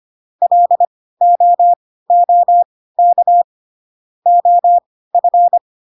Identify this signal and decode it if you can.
Morse code, heard as a keyed tone, about 25 wpm, spaced as for 16 wpm, 700 Hz. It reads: LOOK OF
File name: Pinocchio / Pinocchio ch1 10 25 ik1hge